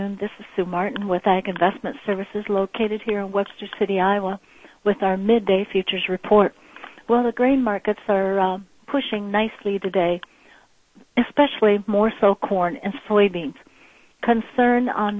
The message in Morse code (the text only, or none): none